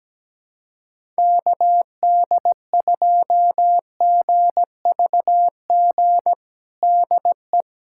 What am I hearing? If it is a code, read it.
Morse code, heard as a keyed tone, about 17 wpm, 700 Hz. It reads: KD2GVG DE
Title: USA Callsign DE Callsign - 6 Characters - 17wpm